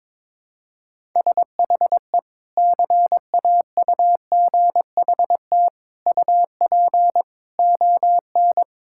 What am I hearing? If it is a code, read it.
Morse code, heard as a keyed tone, about 22 wpm, 700 Hz. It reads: SHE CAUGHT UP ON
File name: practice / 22wpm / Sentences with 3-Word Phrases - 22wpm